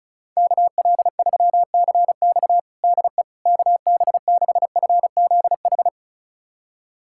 Morse code, heard as a keyed tone, about 34 wpm, 700 Hz. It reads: KL3CX DE KB6FZH
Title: USA Callsign DE Callsign 35wpm